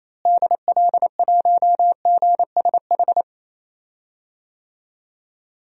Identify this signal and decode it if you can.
Morse code, heard as a keyed tone, about 28 wpm, 700 Hz. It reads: DL1GSH
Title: DXCC-100 Call Signs - 6 Characters - Repeated 2x - 28wpm